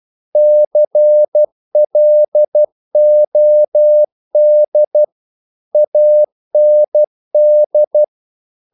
Morse code, heard as a keyed tone, about 12 wpm, 600 Hz. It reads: CLOD AND
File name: mp3 files / the gardener - 12wpm